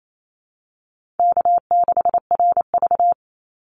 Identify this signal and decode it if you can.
Morse code, heard as a keyed tone, about 28 wpm, 700 Hz. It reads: K6RV
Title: Easy Call Signs - 28wpm